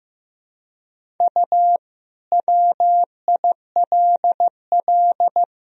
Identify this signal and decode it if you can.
Morse code, heard as a keyed tone, about 15 wpm, 700 Hz. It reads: U WILL
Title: Sentences from Top 400 Words and Abbreviations 15wpm